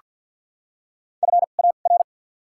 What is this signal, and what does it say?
Morse code, heard as a keyed tone, about 48 wpm, 700 Hz. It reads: F A R
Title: 2x Character Spacing - Top 500 Words - 50wpm